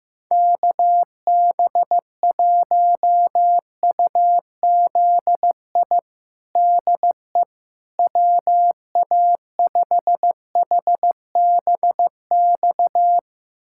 Morse code, heard as a keyed tone, about 15 wpm, 700 Hz. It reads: KB1UZI DE WA5HBX
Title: USA Callsign DE Callsign 15wpm